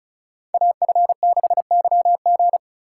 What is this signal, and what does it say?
Morse code, heard as a keyed tone, about 34 wpm, 700 Hz. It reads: AF6YG
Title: USA Call Signs - 5 Characters - 2x Character Spacing - 35wpm